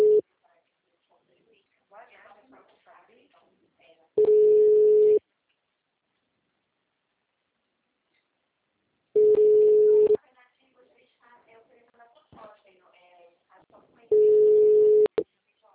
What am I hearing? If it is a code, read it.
Ringback tone (Europe, 425 Hz), with speech.